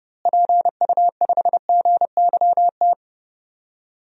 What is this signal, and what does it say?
Morse code, heard as a keyed tone, about 30 wpm, 700 Hz. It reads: PU5GYT
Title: DXCC-100 Call Signs - 6 Characters - Repeated 2x - 30wpm